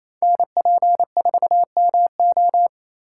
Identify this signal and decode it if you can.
Morse code, heard as a keyed tone, about 30 wpm, 700 Hz. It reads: NP4MO
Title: Speed Racing - USA 2x2 Call Signs 28wpm